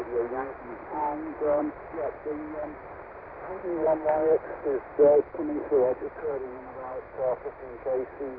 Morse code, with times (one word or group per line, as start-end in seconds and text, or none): none